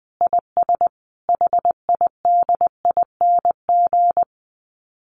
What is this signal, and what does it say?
Morse code, heard as a keyed tone, about 20 wpm, 700 Hz. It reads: IS HIDING
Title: Dit Challenge 20wpm